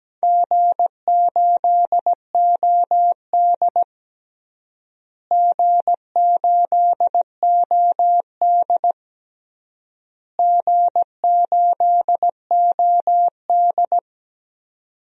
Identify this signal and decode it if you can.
Morse code, heard as a keyed tone, about 17 wpm, 700 Hz. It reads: G8OD G8OD G8OD
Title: DXCC-100 Call Signs - 4 Characters - Repeated 3x - 17wpm